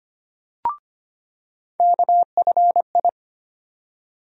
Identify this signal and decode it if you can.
Morse code, heard as a keyed tone, about 25 wpm, 700 Hz. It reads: KFI
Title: USA Call Sign Postfixes - 3 Letters 25wpm